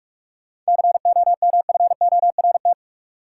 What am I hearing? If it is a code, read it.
Morse code, heard as a keyed tone, about 46 wpm, 700 Hz. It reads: COMFORT
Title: Seven Letter Words - 45wpm